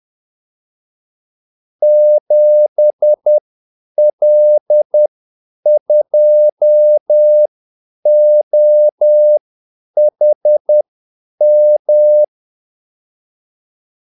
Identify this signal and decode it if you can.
Morse code, heard as a keyed tone, about 10 wpm, spaced as for 8 wpm, 600 Hz. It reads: 7L2OHM
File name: Callsigns-10-8wpm